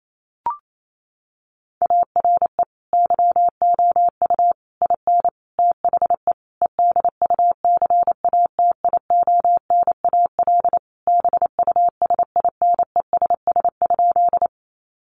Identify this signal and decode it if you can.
Morse code, heard as a keyed tone, about 28 wpm, 700 Hz. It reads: ARE YOU IN THE EDUCATIONAL BUSINESS?
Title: Sentences from Top 1200 Words 28wpm